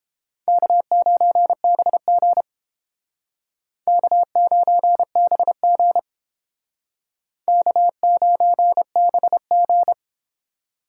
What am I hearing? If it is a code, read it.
Morse code, heard as a keyed tone, about 29 wpm, 700 Hz. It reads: K9BG K9BG K9BG